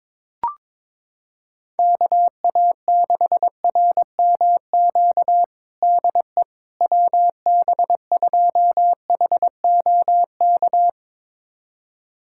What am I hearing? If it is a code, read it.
Morse code, heard as a keyed tone, about 22 wpm, 700 Hz. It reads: KA6RMQ DE WB2HOK
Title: USA Callsign DE Callsign - 6 Characters - 22wpm